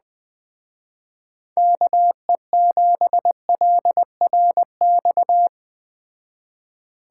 Morse code, heard as a keyed tone, about 20 wpm, 700 Hz. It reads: KE7LRX